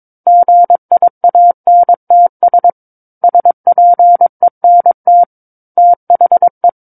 Morse code, heard as a keyed tone, about 22 wpm, 700 Hz. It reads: GIANTS SPENT THE
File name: Book One / 13-Chapter 13 - 22 WPM